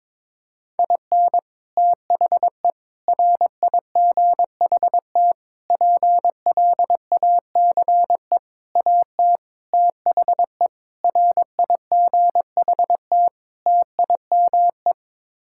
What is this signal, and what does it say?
Morse code, heard as a keyed tone, about 22 wpm, 700 Hz. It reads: IN THE RIGHT PLACE AT THE RIGHT TIME
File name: practice / 22wpm / Sets of 8 Words - 22wpm